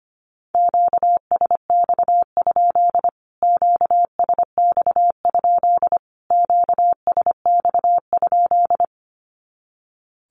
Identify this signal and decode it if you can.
Morse code, heard as a keyed tone, about 25 wpm, 700 Hz. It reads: QSX? QSX? QSX?
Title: Q-Codes Repeated 3x 25wpm